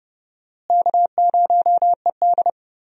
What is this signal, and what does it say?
Morse code, heard as a keyed tone, about 30 wpm, 700 Hz. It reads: K0ED
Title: USA Call Signs - 4 Characters - Repeated 3x - 30wpm